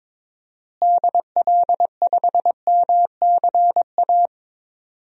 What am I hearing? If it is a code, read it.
Morse code, heard as a keyed tone, about 22 wpm, 700 Hz. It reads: DL5MCA
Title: DXCC-100 Call Signs - 6 Characters - Repeated 2x - 22wpm